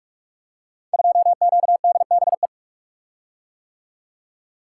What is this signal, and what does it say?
Morse code, heard as a keyed tone, about 44 wpm, 700 Hz. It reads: 2QDBE
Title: Code Groups - 5 Letter:Numbers - 45wpm